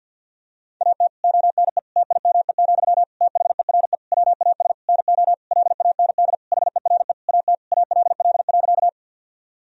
Morse code, heard as a keyed tone, about 50 wpm, 700 Hz. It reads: AT ONE TIME, THERE WAS NO LAND HERE AT ALL.
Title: Sentences from Top 1000 Words - Review Part 1 - 50wpm